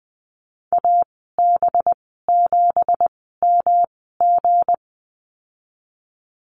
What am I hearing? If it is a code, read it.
Morse code, heard as a keyed tone, about 20 wpm, spaced as for 14 wpm, 700 Hz. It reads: AB7MG
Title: 245 Lesson 39g - Letter z - Call Signs 20wpm